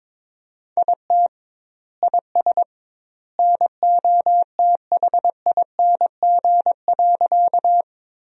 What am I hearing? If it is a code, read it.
Morse code, heard as a keyed tone, about 22 wpm, 700 Hz. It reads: IT IS NOTHING.
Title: 2x Word-Spacing - Sentences from Top 500 Words - 22wpm